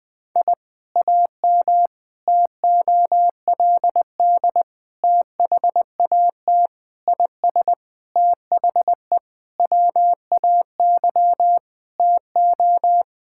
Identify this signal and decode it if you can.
Morse code, heard as a keyed tone, about 21 wpm, 700 Hz. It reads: I AM TOLD THAT IS THE WAY TO
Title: Sentences from Top 1000 Words - Review Part 5 - 20wpm